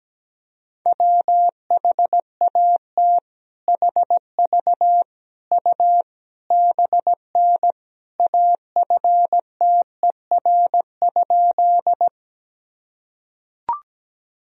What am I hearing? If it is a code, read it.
Morse code, heard as a keyed tone, about 17 wpm, 700 Hz. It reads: WHAT HV U BN AFTER?
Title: Sentences from Top 200 Words and Abbreviations 17wpm